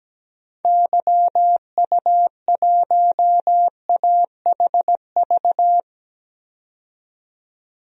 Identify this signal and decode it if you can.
Morse code, heard as a keyed tone, about 17 wpm, 700 Hz. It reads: YU1AHV
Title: DXCC-100 Call Signs - 6 Characters - 17wpm